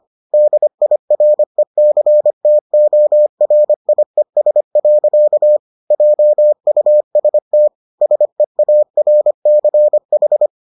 Morse code, heard as a keyed tone, about 25 wpm, 600 Hz. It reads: DIRECTORIES. JUST SEARCH